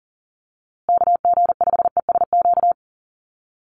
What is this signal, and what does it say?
Morse code, heard as a keyed tone, about 40 wpm, 700 Hz. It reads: KG5ESQ
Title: USA Call Signs Repeated 3x 40wpm